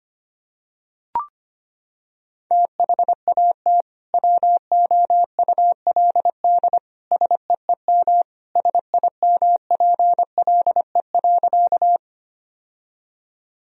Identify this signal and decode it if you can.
Morse code, heard as a keyed tone, about 25 wpm, 700 Hz. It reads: THAT WOULD SEEM SIMPLE.
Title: Sentences from Top 1000 Words - Review Part 1 - 25wpm